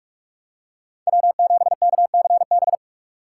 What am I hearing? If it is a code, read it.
Morse code, heard as a keyed tone, about 46 wpm, 700 Hz. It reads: W7KCB
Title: USA Call Signs - 5 Characters - 2x Character Spacing - 45wpm